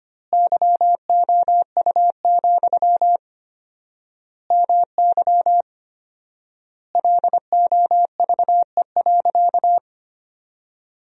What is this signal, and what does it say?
Morse code, heard as a keyed tone, about 25 wpm, 700 Hz. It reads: YOU, MY LOVE.
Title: 4x Word-Spacing - Sentences from Top 400 Words - 25wpm